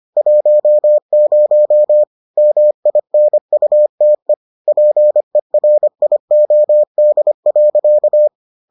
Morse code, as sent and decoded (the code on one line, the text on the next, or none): .---- ----- / -- .. -. ..- - . / .--. .-. .. --- -.. .-.-.-
10 MINUTE PERIOD.